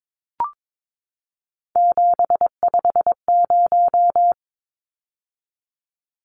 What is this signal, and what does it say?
Morse code, heard as a keyed tone, about 22 wpm, 700 Hz. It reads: 750
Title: Numbers - 3-Digit 22wpm